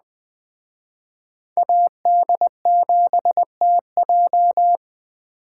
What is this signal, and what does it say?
Morse code, heard as a keyed tone, about 20 wpm, 700 Hz. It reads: AD7TJ
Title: Speed Racing - USA 2x2 Call Signs 17wpm